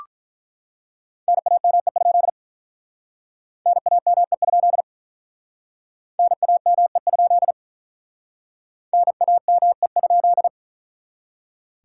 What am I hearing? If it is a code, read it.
Morse code, heard as a keyed tone, about 45 wpm, 700 Hz. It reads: NAME? NAME? NAME? NAME?